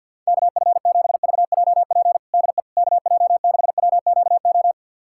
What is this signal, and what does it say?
Morse code, heard as a keyed tone, about 46 wpm, 690 Hz. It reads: KF7VJP DE KJ6WQO
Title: USA Callsign DE Callsign 50wpm